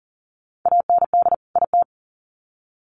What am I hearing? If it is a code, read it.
Morse code, heard as a keyed tone, about 40 wpm, 700 Hz. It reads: AND IT
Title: Sets of 2 Words 40wpm